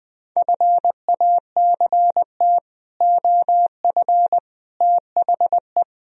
Morse code, heard as a keyed tone, about 20 wpm, 700 Hz. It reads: FACT OF THE